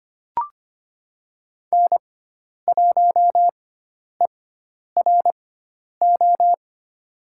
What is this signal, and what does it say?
Morse code, heard as a keyed tone, about 25 wpm, spaced as for 10 wpm, 700 Hz. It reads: N1ERO
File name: Lesson 16g - Letter c - Call Signs 25wpm